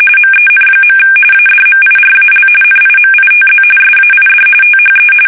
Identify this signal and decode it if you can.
RTTY, about 45.45 baud. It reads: MEN. ADMIRAL NIMITZ SAID THE
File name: fsk850 / battleshipnj